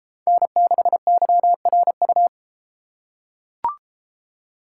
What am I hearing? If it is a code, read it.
Morse code, heard as a keyed tone, about 33 wpm, 700 Hz. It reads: N6YRU